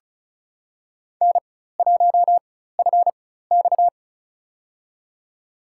Morse code, heard as a keyed tone, about 34 wpm, spaced as for 16 wpm, 700 Hz. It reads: N1FX
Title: USA Call Signs - 4 Characters - 4x Character Spacing - 35wpm